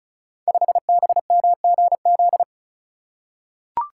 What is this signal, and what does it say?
Morse code, heard as a keyed tone, about 36 wpm, 700 Hz. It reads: 5BMGZ
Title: Code Groups - 5 Letter:Numbers - 35wpm